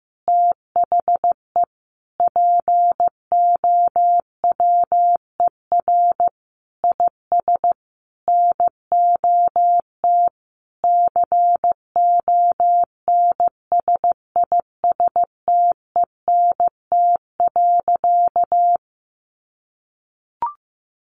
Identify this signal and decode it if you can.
Morse code, heard as a keyed tone, about 15 wpm, 700 Hz. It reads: THE POWER IS NOT CONSISTENT.
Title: Sentences from Top 1500 Words 15wpm